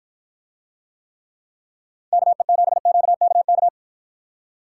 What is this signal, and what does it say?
Morse code, heard as a keyed tone, about 53 wpm, 700 Hz. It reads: KE7QKK